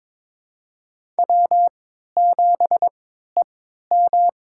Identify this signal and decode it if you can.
Morse code, heard as a keyed tone, about 22 wpm, spaced as for 13 wpm, 700 Hz. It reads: W7EM
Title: USA Call Signs - 4 Characters - 3x Character Spacing - 22wpm